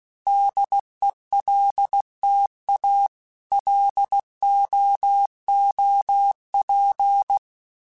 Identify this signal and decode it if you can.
Morse code, heard as a keyed tone, about 16 wpm, 800 Hz. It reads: DELTA LOOP
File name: fromSMs / FromSMs mp3s / Antennas@15